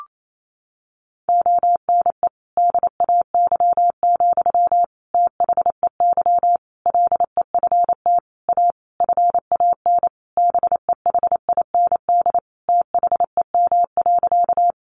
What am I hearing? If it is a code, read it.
Morse code, heard as a keyed tone, about 28 wpm, 700 Hz. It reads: ONE DAY, THEY LEFT A FAN BEHIND THEM.
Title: Sentences from Top 1300 Words 28wpm